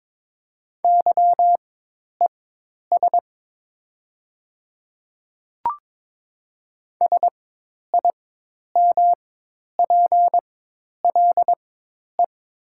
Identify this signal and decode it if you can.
Morse code, heard as a keyed tone, about 22 wpm, spaced as for 10 wpm, 700 Hz. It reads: YES SIMPLE
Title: 4x Character Spacing - Top 500 Words - 22wpm